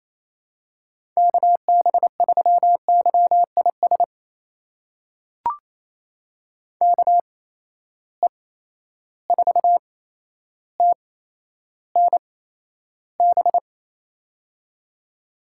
Morse code, heard as a keyed tone, about 28 wpm, spaced as for 12 wpm, 700 Hz. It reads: KB3YIS KE4TNB
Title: USA Call Signs - 6 Characters - 8x Character Spacing - 28wpm